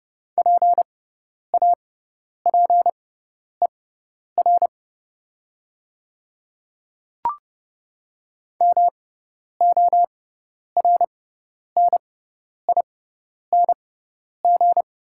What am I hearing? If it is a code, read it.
Morse code, heard as a keyed tone, about 30 wpm, spaced as for 10 wpm, 690 Hz. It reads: PAPER MORNING
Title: 6x Character Spacing - Top 500 Words - 30wpm